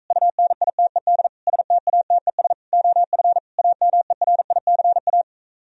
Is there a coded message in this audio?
Yes: Morse code from 0.1 to 5.2 s, heard as a keyed tone, about 46 wpm, 700 Hz. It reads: UNITED STATES OF AMERICA